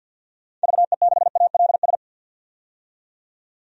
Morse code, heard as a keyed tone, about 52 wpm, 700 Hz. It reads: VE6ALS
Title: DXCC-100 Call Signs - 6 Characters - 50wpm